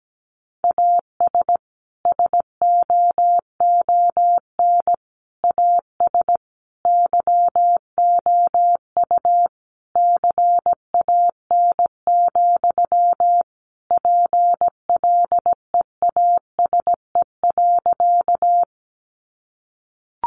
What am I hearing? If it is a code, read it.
Morse code, heard as a keyed tone, about 17 wpm, 700 Hz. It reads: AS SOON AS YOU CAN, PLEASE.
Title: Sentences from Top 1500 Words 17wpm